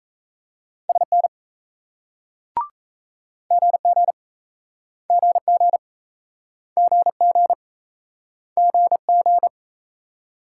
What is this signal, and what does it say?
Morse code, heard as a keyed tone, about 33 wpm, 700 Hz. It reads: SN GG GG GG GG
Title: Speed-Racing Multi-letter Phonemes - 28wpm